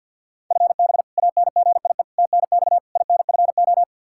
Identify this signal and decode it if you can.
Morse code, heard as a keyed tone, about 53 wpm, 700 Hz. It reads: FB ANGIE TNX INFO
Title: K1USN SST QSO Practice 50wpm